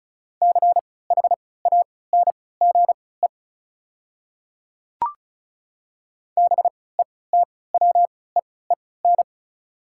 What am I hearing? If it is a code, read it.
Morse code, heard as a keyed tone, about 36 wpm, spaced as for 20 wpm, 700 Hz. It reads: CHANGE BETWEEN
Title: 3x Character Spacing - Top 300 Words - 35wpm